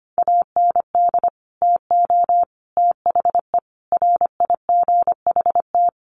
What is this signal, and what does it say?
Morse code, heard as a keyed tone, about 25 wpm, 700 Hz. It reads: AND TO THE RIGHT
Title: Sentences from Top 1300 Words 25wpm